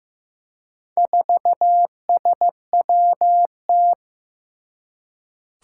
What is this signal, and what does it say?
Morse code, heard as a keyed tone, about 15 wpm, 700 Hz. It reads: 4SWT
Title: Four Challenging Letter-Number Combinations 15wpm